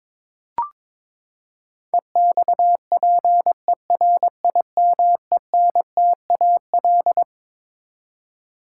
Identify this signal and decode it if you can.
Morse code, heard as a keyed tone, about 21 wpm, 700 Hz. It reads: EXPERIMENTAL
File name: Twelve Letter Words - 22wpm